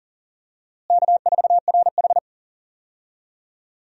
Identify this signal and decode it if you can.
Morse code, heard as a keyed tone, about 40 wpm, 700 Hz. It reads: K4RH
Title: USA Call Signs - 4 Characters - Repeated 2x - 40wpm